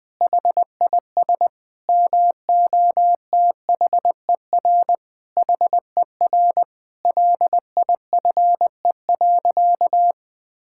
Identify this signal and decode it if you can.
Morse code, heard as a keyed tone, about 20 wpm, 700 Hz. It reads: HIS MOTHER HER LIFE.
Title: Sentences from Top 1000 Words - Review Part 5 - 20wpm